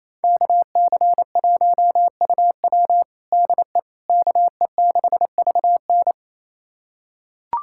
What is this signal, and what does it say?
Morse code, heard as a keyed tone, about 28 wpm, 700 Hz. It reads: KC1UW DE KE6VN
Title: USA Callsign DE Callsign - 5 Characters - 28wpm